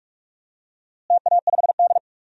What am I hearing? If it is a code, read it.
Morse code, heard as a keyed tone, about 46 wpm, 700 Hz. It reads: TA5D
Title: DXCC-100 Call Signs - 4 Characters - Repeated 2x - 45wpm